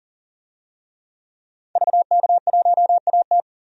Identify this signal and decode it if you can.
Morse code, heard as a keyed tone, about 40 wpm, 700 Hz. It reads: VK1AT
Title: DXCC-100 Call Signs - 5 Characters - 40wpm